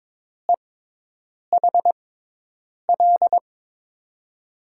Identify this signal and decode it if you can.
Morse code, heard as a keyed tone, about 22 wpm, spaced as for 8 wpm, 700 Hz. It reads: EHL
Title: USA Call Signs - 5 Characters - 6x Character Spacing - 22wpm